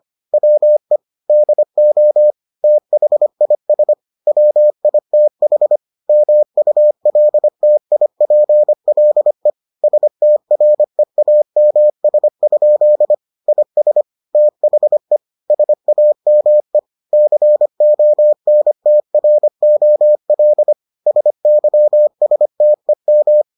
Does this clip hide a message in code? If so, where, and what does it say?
Yes: Morse code from 0.3 to 23.4 s, heard as a keyed tone, about 25 wpm, 600 Hz. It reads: WE DO THIS WITH MULTIPLE STREAMS? IS THE SAME CONTROL SYSTEM